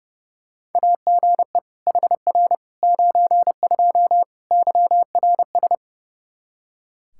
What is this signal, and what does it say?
Morse code, heard as a keyed tone, about 31 wpm, 700 Hz. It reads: AGE HR 92 YRS
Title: Practice QSO - Exchange 3 - More Info - 30wpm